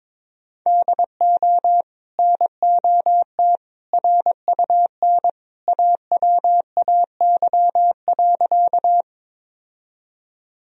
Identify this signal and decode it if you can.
Morse code, heard as a keyed tone, about 22 wpm, 700 Hz. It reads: DO NOT RUN AWAY.
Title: Sentences from Top 1000 Words - Review Part 1 - 22wpm